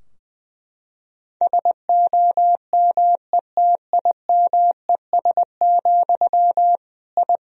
Morse code, heard as a keyed tone, about 20 wpm, 700 Hz. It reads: SOMETIMES, I
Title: Sentences from Top 1000 Words - Review Part 3 - 20wpm